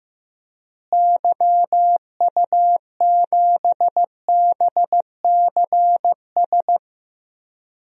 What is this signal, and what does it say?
Morse code, heard as a keyed tone, about 15 wpm, 700 Hz. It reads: YU7BCS
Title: DXCC-100 Call Signs - 6 Characters - Repeated 2x - 15wpm